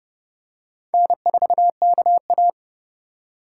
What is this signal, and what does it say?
Morse code, heard as a keyed tone, about 28 wpm, 700 Hz. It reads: N4KA